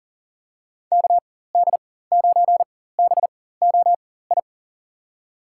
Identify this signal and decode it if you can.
Morse code, heard as a keyed tone, about 40 wpm, spaced as for 19 wpm, 700 Hz. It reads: KD9BOI
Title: USA Call Signs - 4x Character Spacing - 40wpm